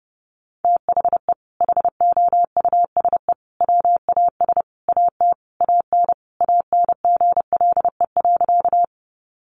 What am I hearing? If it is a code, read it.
Morse code, heard as a keyed tone, about 30 wpm, 700 Hz. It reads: THE HOUSE WAS AT AN ANGLE.
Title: Sentences from Top 1500 Words 30wpm